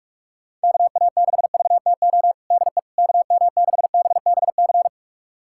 Morse code, heard as a keyed tone, about 46 wpm, 700 Hz. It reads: KA6VTO DE KM6BBC
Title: USA Callsign DE Callsign 45wpm